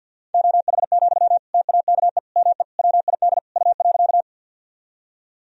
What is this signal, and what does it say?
Morse code, heard as a keyed tone, about 52 wpm, 700 Hz. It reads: OH, TAKE ME WID U.
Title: Sentences from Top 500 Words and Abbreviations 50wpm